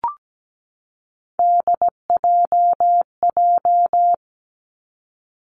Morse code, heard as a keyed tone, about 17 wpm, 700 Hz. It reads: DJJ
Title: Three Challenging Letter-Number Combinations 17wpm